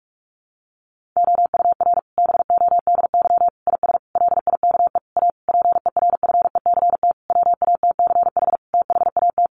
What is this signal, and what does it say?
Morse code, heard as a keyed tone, about 44 wpm, 700 Hz. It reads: OUR BODY IS LIKE A PERFECT WATCH THAT
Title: Sentences from Top 1000 Words 45wpm